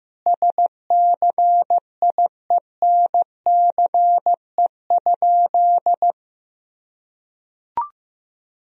Morse code, heard as a keyed tone, about 16 wpm, 700 Hz. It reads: SCIENCE?
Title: Sentences from Top 1000 Words - Review Part 2 - 15wpm